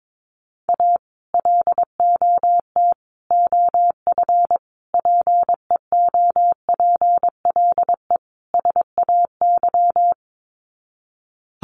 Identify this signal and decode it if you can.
Morse code, heard as a keyed tone, about 22 wpm, 700 Hz. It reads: A LOT OF PEOPLE SAY
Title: Sets of 5 Words Encore2 22wpm